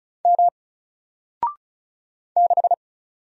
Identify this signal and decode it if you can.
Morse code, heard as a keyed tone, about 36 wpm, 700 Hz. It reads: M 6
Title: Single Character (Letter:Number:Punctuation:Prosign) Alt 35wpm